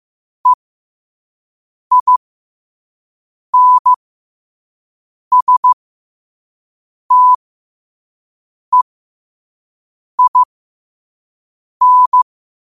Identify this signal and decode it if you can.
Morse code, heard as a keyed tone, about 15 wpm, spaced as for 5 wpm, 1000 Hz. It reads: EINSTEIN
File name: 5wpm